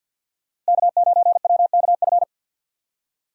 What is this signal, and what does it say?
Morse code, heard as a keyed tone, about 46 wpm, 700 Hz. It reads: K9WKF
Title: USA Call Signs - 5 Characters - 2x Character Spacing - 50wpm